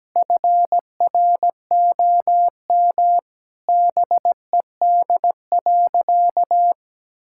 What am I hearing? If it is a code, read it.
Morse code, heard as a keyed tone, about 17 wpm, 700 Hz. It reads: FROM BED.
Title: Sentences from Top 1000 Words - Review Part 6 - 17wpm